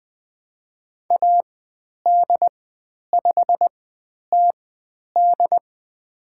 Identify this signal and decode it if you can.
Morse code, heard as a keyed tone, about 20 wpm, spaced as for 10 wpm, 700 Hz. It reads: AD5TD
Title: 095 Lesson 16g - Letter c - Call Signs 20wpm